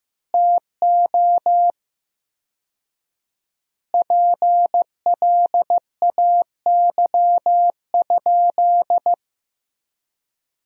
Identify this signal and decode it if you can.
Morse code, heard as a keyed tone, about 15 wpm, 700 Hz. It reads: TO PLAY?
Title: 4x Word-Spacing - Sentences from Top 400 Words - 15wpm